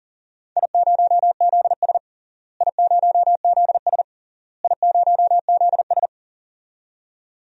Morse code, heard as a keyed tone, about 40 wpm, 700 Hz. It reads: I0ZS I0ZS I0ZS
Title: DXCC-100 Call Signs - 4 Characters - Repeated 3x - 40wpm